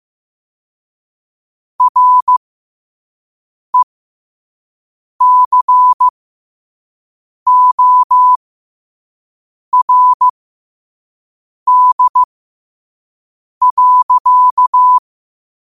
Morse code, heard as a keyed tone, about 15 wpm, spaced as for 5 wpm, 1000 Hz. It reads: RECORD.
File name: qotdfiles / 5wpm